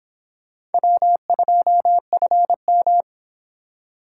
Morse code, heard as a keyed tone, about 30 wpm, 700 Hz. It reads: W2FM